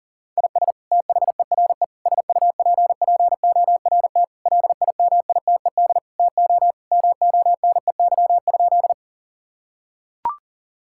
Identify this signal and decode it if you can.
Morse code, heard as a keyed tone, about 40 wpm, 700 Hz. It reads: IS THERE SUPPORT LIMITED TO MONEY?
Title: Sentences from Top 1000 Words - Review Part 6 - 40wpm